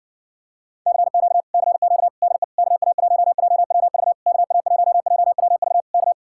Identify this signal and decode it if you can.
Morse code, heard as a keyed tone, about 59 wpm, 700 Hz. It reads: CQ CQ DE KA1JWV KA1JWV K